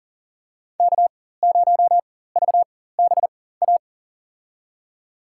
Morse code, heard as a keyed tone, about 40 wpm, spaced as for 19 wpm, 700 Hz. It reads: K0VBA
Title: USA Call Signs - 4x Character Spacing - 40wpm